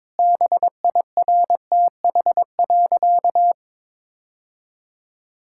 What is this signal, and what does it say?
Morse code, heard as a keyed tone, about 22 wpm, 700 Hz. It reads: BIRTH.
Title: Sentences from Top 1000 Words - Review Part 5 - 22wpm